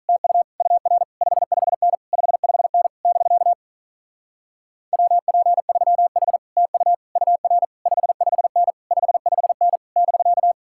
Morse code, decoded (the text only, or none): TU UR 55N 55N <BK> WP3H TU UR 55N 55N <BK>